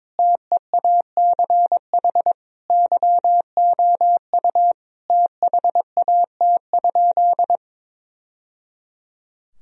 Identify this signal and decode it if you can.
Morse code, heard as a keyed tone, about 22 wpm, 700 Hz. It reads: TEACH YOU THAT?
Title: Sentences from Top 1000 Words - Review Part 2 - 22wpm